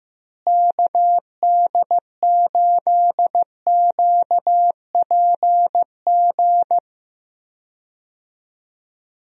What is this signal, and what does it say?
Morse code, heard as a keyed tone, about 15 wpm, 700 Hz. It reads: KD8QPG
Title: USA Call Signs - 6 Characters - Repeated 3x - 15wpm